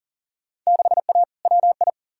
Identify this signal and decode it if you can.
Morse code, heard as a keyed tone, about 40 wpm, 700 Hz. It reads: 6A WI
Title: Field Day Exchanges - 40wpm